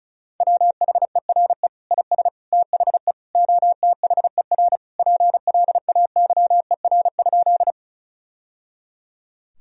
Morse code, heard as a keyed tone, about 34 wpm, 700 Hz. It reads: WHERE IS THE OTHER PLAYER?
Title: Sentences from Top 1000 Words - Review Part 4 - 35wpm